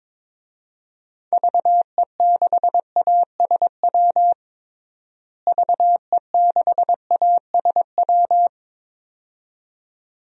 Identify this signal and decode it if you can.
Morse code, heard as a keyed tone, about 22 wpm, 700 Hz. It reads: VE6ASW VE6ASW
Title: DXCC-100 Call Signs - 6 Characters - Repeated 2x - 22wpm